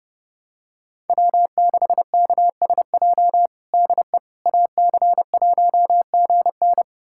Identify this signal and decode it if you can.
Morse code, heard as a keyed tone, about 30 wpm, 700 Hz. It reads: W6KSJ DE AC1GN
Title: USA Callsign DE Callsign - 5 Characters - 30wpm